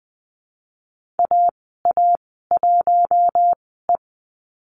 Morse code, heard as a keyed tone, about 20 wpm, spaced as for 15 wpm, 700 Hz. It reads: AA1E
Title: 048 Lesson 9g - Number 4 - Call Signs 20wpm